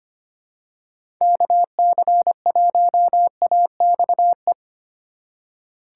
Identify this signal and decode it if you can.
Morse code, heard as a keyed tone, about 25 wpm, 700 Hz. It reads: KC1AXE
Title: USA Call Signs - 6 Characters - 4x Character Spacing - 25wpm